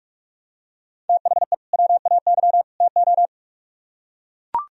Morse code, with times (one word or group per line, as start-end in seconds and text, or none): 1.1-1.5 THE
1.7-2.6 WAY
2.8-3.3 TO